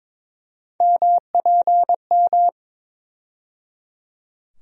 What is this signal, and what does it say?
Morse code, heard as a keyed tone, about 22 wpm, 700 Hz. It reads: MPM
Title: USA Call Sign Postfixes - 3 Letters 22wpm